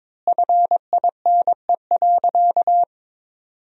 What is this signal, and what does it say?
Morse code, heard as a keyed tone, about 22 wpm, 700 Hz. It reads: FINE.
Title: 4x Word-Spacing - Sentences from Top 500 Words - 22wpm